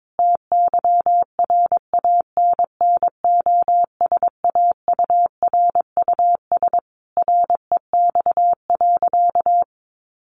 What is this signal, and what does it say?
Morse code, heard as a keyed tone, about 22 wpm, 700 Hz. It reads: TYRANNOSAURUS REX.
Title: A Sound Of Thunder 22wpm